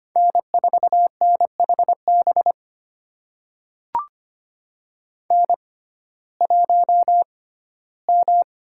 Morse code, heard as a keyed tone, about 25 wpm, spaced as for 15 wpm, 700 Hz. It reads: N4NHB N1M
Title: USA Call Signs - 6x Character Spacing - 25wpm